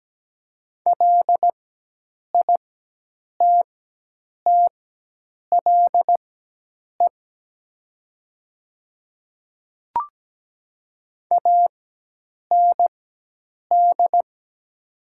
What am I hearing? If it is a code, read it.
Morse code, heard as a keyed tone, about 17 wpm, spaced as for 8 wpm, 700 Hz. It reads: LITTLE AND
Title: 4x Character Spacing - Top 300 Words - 17wpm